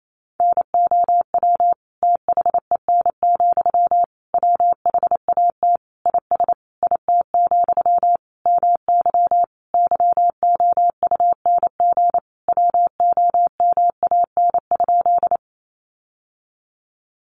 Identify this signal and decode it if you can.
Morse code, heard as a keyed tone, about 28 wpm, 700 Hz. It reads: NOW THEN, WHAT IS IT, MY YOUNG WOMAN?
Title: Sentences from Top 1100 Words 28wpm